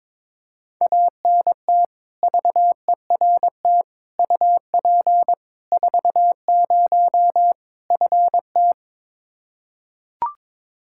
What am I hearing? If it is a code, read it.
Morse code, heard as a keyed tone, about 22 wpm, 700 Hz. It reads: ANT VERT UP 40 FT
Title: Practice QSO - Exchange 2 - Info - 22wpm